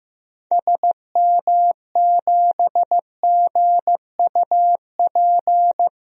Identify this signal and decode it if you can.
Morse code, heard as a keyed tone, about 16 wpm, 700 Hz. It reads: SM7GUP
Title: DXCC-100 Call Signs - 6 Characters - Repeated 2x - 15wpm